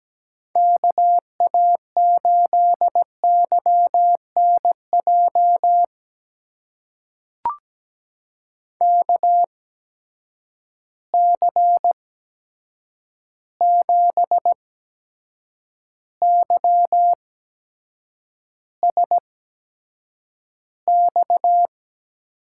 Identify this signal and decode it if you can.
Morse code, heard as a keyed tone, about 17 wpm, spaced as for 7 wpm, 700 Hz. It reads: KA8YNJ KC7YSX
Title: USA Call Signs - 6 Characters - 8x Character Spacing - 17wpm